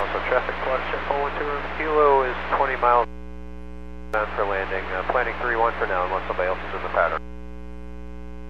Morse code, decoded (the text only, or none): none